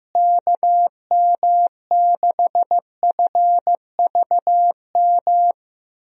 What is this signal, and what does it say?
Morse code, heard as a keyed tone, about 15 wpm, 700 Hz. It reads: KM6FVM
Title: USA Call Signs - 6 Characters - Repeated 2x - 15wpm